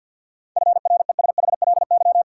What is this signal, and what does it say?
Morse code, heard as a keyed tone, about 48 wpm, 700 Hz. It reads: FRESHLY
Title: Sets of 5 Words Encore - New Word Drill - 50wpm